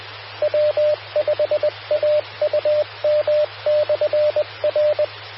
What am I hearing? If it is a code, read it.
Morse code, heard as a keyed tone, about 20 wpm, 600 Hz, with radio noise. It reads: W5AUM/R
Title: Austin, TX Amateur Repeaters